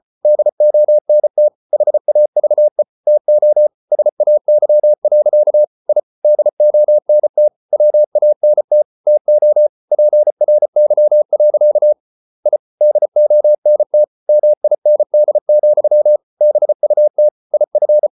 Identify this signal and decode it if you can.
Morse code, heard as a keyed tone, about 34 wpm, 600 Hz. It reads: DONT HAVE TO SAY. I DONT WANT TO PRY. I DONT MIND, BUT IF